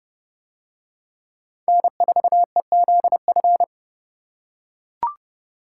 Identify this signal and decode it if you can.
Morse code, heard as a keyed tone, about 26 wpm, 700 Hz. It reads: N4EZF